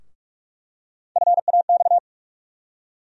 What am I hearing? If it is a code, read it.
Morse code, heard as a keyed tone, about 46 wpm, 700 Hz. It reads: FAX
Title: USA Call Sign Postfixes - 3 Letters 45wpm